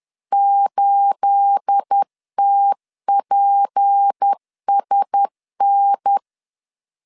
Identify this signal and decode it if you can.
Morse code, heard as a keyed tone, about 10 wpm, 800 Hz. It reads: 8TPSN
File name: LEZIONI / 10 SIGNS / 10A SLASH 10wpm 2808